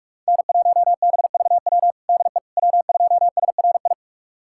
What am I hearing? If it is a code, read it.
Morse code, heard as a keyed tone, about 42 wpm, 700 Hz. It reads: N1BVW DE W2SRI
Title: USA Callsign DE Callsign - 5 Characters - 45wpm